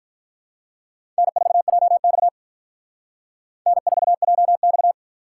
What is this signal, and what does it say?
Morse code, heard as a keyed tone, about 45 wpm, 700 Hz. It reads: N4JX N4JX